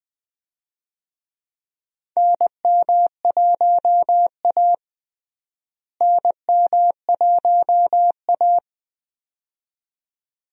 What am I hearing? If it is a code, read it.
Morse code, heard as a keyed tone, about 21 wpm, 700 Hz. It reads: NM1A NM1A
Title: USA Call Signs - 4 Characters - Repeated 2x - 20wpm